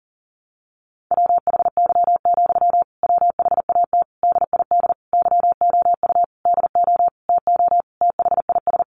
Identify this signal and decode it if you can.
Morse code, heard as a keyed tone, about 37 wpm, 700 Hz. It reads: WHY, WHAT DID YOU DO TO THIS
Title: Sentences from Top 1500 Words 40wpm